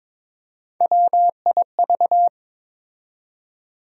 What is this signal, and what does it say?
Morse code, heard as a keyed tone, about 22 wpm, 700 Hz. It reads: WIV
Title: USA Call Sign Postfixes - 3 Letters 22wpm